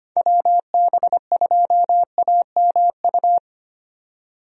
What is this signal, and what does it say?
Morse code, heard as a keyed tone, about 25 wpm, 700 Hz. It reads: WB2AMU
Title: USA Call Signs - 6 Characters - 25wpm